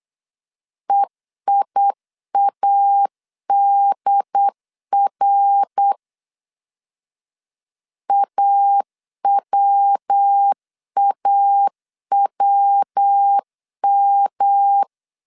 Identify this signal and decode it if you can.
Morse code, heard as a keyed tone, about 8 wpm, 800 Hz. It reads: EIADR AWAWM
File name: CW / DET / 1 3 1500C S C 8wpm 4214